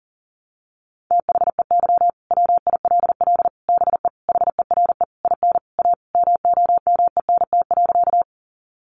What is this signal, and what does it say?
Morse code, heard as a keyed tone, about 40 wpm, 700 Hz. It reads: THEY WILL BE HERE IN A MOMENT.
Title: Sentences from Top 1000 Words 40wpm